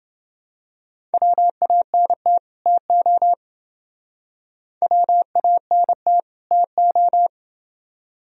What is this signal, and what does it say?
Morse code, heard as a keyed tone, about 30 wpm, 700 Hz. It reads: WANT TO WANT TO